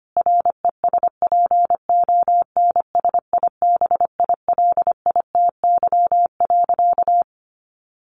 Morse code, heard as a keyed tone, about 25 wpm, 700 Hz. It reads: RESPONSIBILITY.
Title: Sentences from Top 1500 Words 25wpm